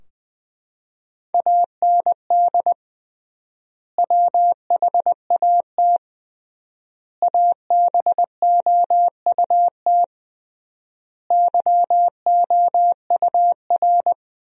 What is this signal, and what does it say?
Morse code, heard as a keyed tone, about 20 wpm, 700 Hz. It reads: AND WHAT ABOUT YOUR
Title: 3x Word-Spacing - Sentences from Top 200 Words - 20wpm